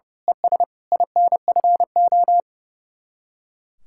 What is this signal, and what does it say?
Morse code, heard as a keyed tone, about 34 wpm, 700 Hz. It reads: ES INFO
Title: Speed-Racing Common QSO Phrases - 30wpm